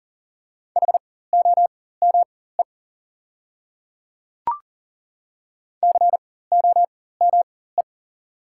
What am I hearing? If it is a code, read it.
Morse code, heard as a keyed tone, about 40 wpm, spaced as for 19 wpm, 700 Hz. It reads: HOME COME
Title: 4x Character Spacing - Top 200 Words - 40wpm